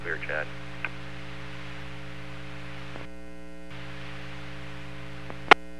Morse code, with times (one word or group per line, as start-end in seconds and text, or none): none